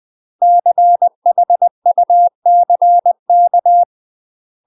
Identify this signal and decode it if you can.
Morse code, heard as a keyed tone, about 20 wpm, 700 Hz. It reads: CHUCK
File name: UK MASTER QSOs TRAINING/20WPM